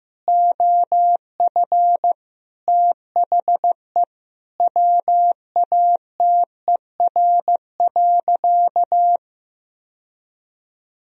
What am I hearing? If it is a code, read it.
Morse code, heard as a keyed tone, about 16 wpm, 700 Hz. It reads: OF THE WATER.
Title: Sentences from Top 1000 Words - Review Part 3 - 15wpm